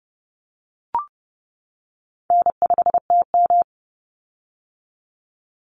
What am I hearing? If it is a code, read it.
Morse code, heard as a keyed tone, about 32 wpm, 700 Hz. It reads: N5TM
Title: Easy Call Signs - 30wpm